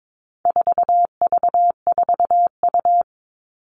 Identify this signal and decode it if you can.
Morse code, heard as a keyed tone, about 22 wpm, 700 Hz. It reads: 4V4U
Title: All Cut Numbers - 4-Digit 22wpm